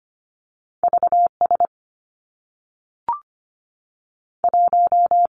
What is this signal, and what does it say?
Morse code, heard as a keyed tone, about 25 wpm, 700 Hz. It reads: V S 1
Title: Two Challenging Letter-Number Combinations 25wpm